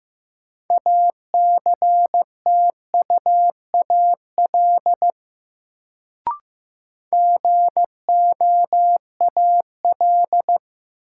Morse code, heard as a keyed tone, about 15 wpm, 700 Hz. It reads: ACTUAL GOAL
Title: Sentences from Top 1000 Words - Review Part 5 - New Word Drill - 15wpm